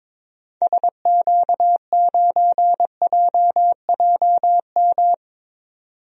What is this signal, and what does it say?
Morse code, heard as a keyed tone, about 22 wpm, 700 Hz. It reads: SQ9JJM
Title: DXCC-100 Call Signs - 6 Characters - 22wpm